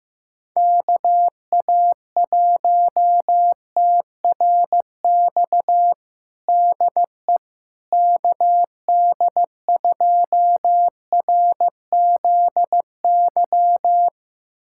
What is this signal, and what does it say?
Morse code, heard as a keyed tone, about 16 wpm, 700 Hz. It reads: KA1TRX DE KD2RZY
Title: USA Callsign DE Callsign - 6 Characters - 15wpm